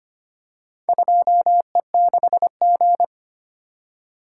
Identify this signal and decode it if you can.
Morse code, heard as a keyed tone, about 25 wpm, 700 Hz. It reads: 2E6G
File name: practice / 25wpm / Four Letter-Number Combinations - 25wpm